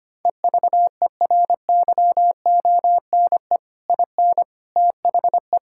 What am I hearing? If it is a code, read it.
Morse code, heard as a keyed tone, about 25 wpm, 700 Hz. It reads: EVERYONE IN THE
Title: Sentences from Top 1000 Words - Review Part 6 - 25wpm